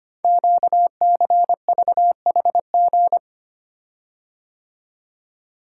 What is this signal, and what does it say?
Morse code, heard as a keyed tone, about 25 wpm, 700 Hz. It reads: QCVHG
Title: Code Groups - 5 Letters - 25wpm